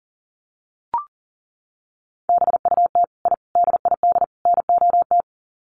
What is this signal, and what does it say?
Morse code, heard as a keyed tone, about 40 wpm, 700 Hz. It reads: BUT I DID NOT
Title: Sets of 4 Words 40wpm